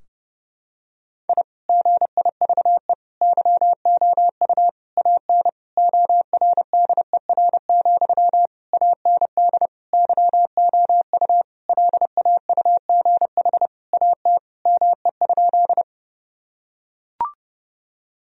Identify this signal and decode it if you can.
Morse code, heard as a keyed tone, about 30 wpm, 700 Hz. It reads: I GIVE YOU AN ORDER, AND YOU LAUGH AT ME?
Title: Sentences from Top 1000 Words - Review Part 2 - 30wpm